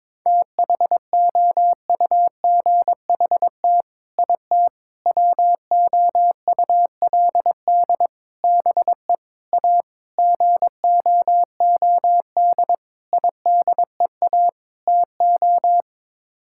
Morse code, heard as a keyed tone, about 22 wpm, 700 Hz. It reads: THOUGHT IT WOULD BE A GOOD IDEA TO
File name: practice / 22wpm / Sets of 8 Words - 22wpm